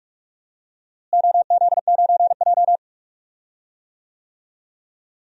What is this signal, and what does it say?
Morse code, heard as a keyed tone, about 46 wpm, 700 Hz. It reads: OZ9J
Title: DXCC-100 Call Signs - 4 Characters - Repeated 2x - 45wpm